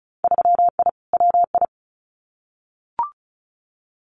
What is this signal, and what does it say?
Morse code, heard as a keyed tone, about 36 wpm, 700 Hz. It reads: 3I WI
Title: Winter Field Day Exchanges 35wpm